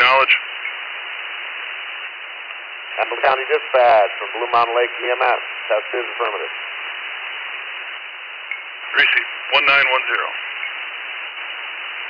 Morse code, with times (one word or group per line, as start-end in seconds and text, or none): none